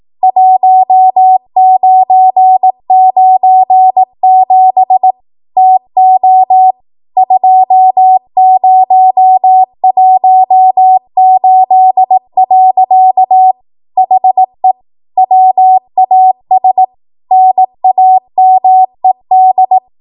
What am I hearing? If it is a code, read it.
Morse code, heard as a keyed tone, about 18 wpm, 750 Hz. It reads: 1997 TO 2018. HE WAS NAMED